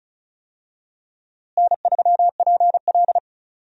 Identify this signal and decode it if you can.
Morse code, heard as a keyed tone, about 36 wpm, 700 Hz. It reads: N3PL
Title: USA Call Signs - 4 Characters - Repeated 3x - 35wpm